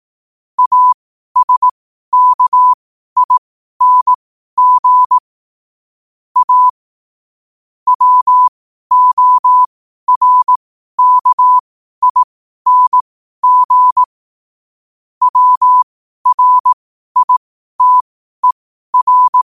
Morse code, heard as a keyed tone, about 18 wpm, spaced as for 12 wpm, 1000 Hz. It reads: ASKING A WORKING WRITER
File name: qotdfiles / 12wpm